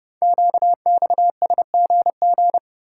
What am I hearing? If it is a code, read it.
Morse code, heard as a keyed tone, about 30 wpm, 700 Hz. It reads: QXSGG
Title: Code Groups - 5 Letter:Numbers - 30wpm